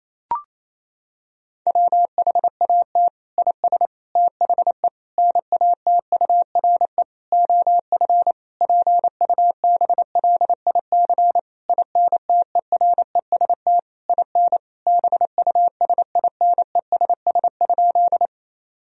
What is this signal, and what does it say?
Morse code, heard as a keyed tone, about 28 wpm, 700 Hz. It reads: WHAT IS THE NATURE OF PUBLIC INTEREST IN BUSINESS?
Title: Sentences from Top 1000 Words - Review Part 3 - 28wpm